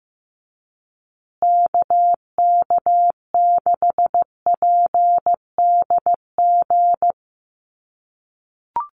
Morse code, heard as a keyed tone, about 15 wpm, 700 Hz. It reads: KK6PDG
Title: USA Call Signs Repeated 3x 15wpm